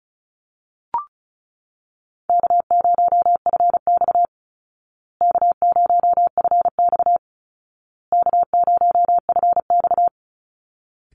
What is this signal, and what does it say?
Morse code, heard as a keyed tone, about 34 wpm, 700 Hz. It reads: K0FX K0FX K0FX
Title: Easy Call Signs Repeated 3x - 35wpm